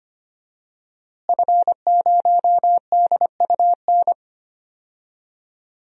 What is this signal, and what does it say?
Morse code, heard as a keyed tone, about 25 wpm, 700 Hz. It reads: F0DUN
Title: DXCC-100 Call Signs - 5 Characters - Repeated 2x - 25wpm